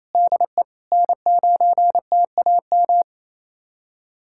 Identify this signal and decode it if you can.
Morse code, heard as a keyed tone, about 27 wpm, 700 Hz. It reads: DE N9TAM
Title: USA Callsign DE Callsign - 5 Characters - 28wpm